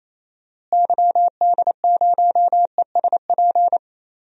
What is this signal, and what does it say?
Morse code, heard as a keyed tone, about 27 wpm, 700 Hz. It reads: YD0ESP
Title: DXCC-100 Call Signs - 6 Characters - Repeated 2x - 28wpm